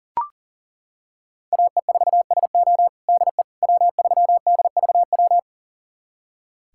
Morse code, heard as a keyed tone, about 40 wpm, 700 Hz. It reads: AE4SO DE W3DVW
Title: USA Callsign DE Callsign - 5 Characters - 40wpm